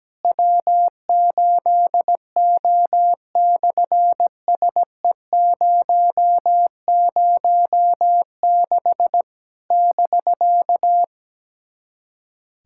Morse code, heard as a keyed tone, about 17 wpm, 700 Hz. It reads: W8O/SE006 <BK>
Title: SOTA QSO Practice 17wpm